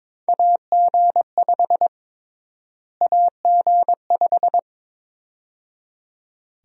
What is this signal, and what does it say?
Morse code, heard as a keyed tone, about 21 wpm, 700 Hz. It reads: AG5 AG5
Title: USA Call Sign Prefixes - Repeated 2x - 22wpm